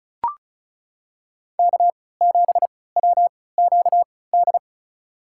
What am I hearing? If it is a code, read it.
Morse code, heard as a keyed tone, about 36 wpm, spaced as for 20 wpm, 700 Hz. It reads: K7WQD
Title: USA Call Signs - 5 Characters - 3x Character Spacing - 35wpm